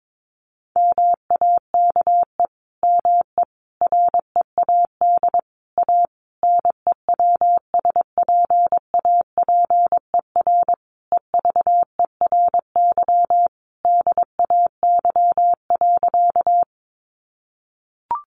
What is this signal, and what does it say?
Morse code, heard as a keyed tone, about 22 wpm, 700 Hz. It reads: MAKE ME READ A NEWSPAPER EVERY DAY.
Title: Sentences from Top 1100 Words 22wpm